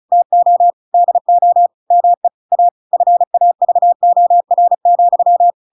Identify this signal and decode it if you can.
Morse code, heard as a keyed tone, about 34 wpm, 700 Hz. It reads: TO DO ME A FAVOR,